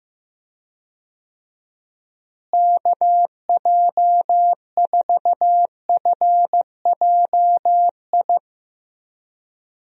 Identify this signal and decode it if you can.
Morse code, heard as a keyed tone, about 15 wpm, 700 Hz. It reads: KJ4FJI
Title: USA Call Signs - 6 Characters - Repeated 2x - 15wpm